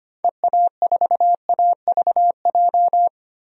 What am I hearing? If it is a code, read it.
Morse code, heard as a keyed tone, about 25 wpm, 700 Hz. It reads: EA4AVJ
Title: DXCC-100 Call Signs - 6 Characters - 25wpm